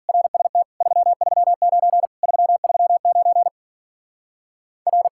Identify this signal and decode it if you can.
Morse code, heard as a keyed tone, about 52 wpm, 700 Hz. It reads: RST 339 339 R